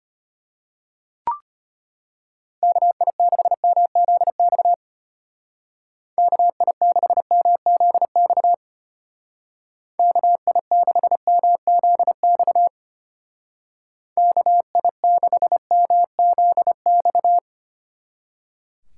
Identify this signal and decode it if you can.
Morse code, heard as a keyed tone, about 32 wpm, 700 Hz. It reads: KI6MZX KI6MZX KI6MZX KI6MZX